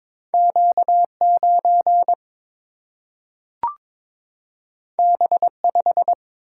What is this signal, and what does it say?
Morse code, heard as a keyed tone, about 22 wpm, 700 Hz. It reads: Q 9 B 5
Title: Single-Letter Single-Number 22wpm